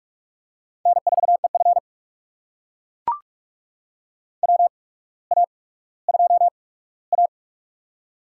Morse code, heard as a keyed tone, about 44 wpm, spaced as for 19 wpm, 700 Hz. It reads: N4EF WA2A
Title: USA Call Signs - 4 Characters - 8x Character Spacing - 45wpm